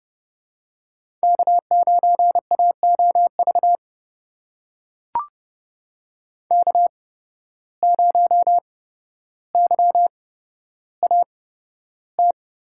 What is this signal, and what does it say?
Morse code, heard as a keyed tone, about 30 wpm, spaced as for 13 wpm, 700 Hz. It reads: K9AOV K0YAT
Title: USA Call Signs - 5 Characters - 8x Character Spacing - 30wpm